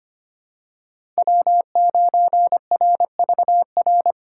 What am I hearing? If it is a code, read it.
Morse code, heard as a keyed tone, about 25 wpm, 700 Hz. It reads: W9RVR
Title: USA Call Signs - 5 Characters - 3x Character Spacing - 25wpm